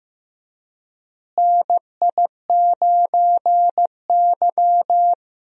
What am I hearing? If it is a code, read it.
Morse code, heard as a keyed tone, about 15 wpm, 700 Hz. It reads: NI9Y
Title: USA Call Signs - 4 Characters - Repeated 3x - 15wpm